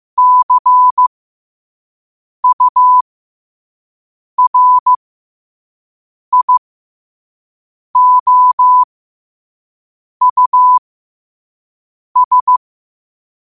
Morse code, heard as a keyed tone, about 15 wpm, spaced as for 5 wpm, 1000 Hz. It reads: CURIOUS
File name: qotdfiles / 5wpm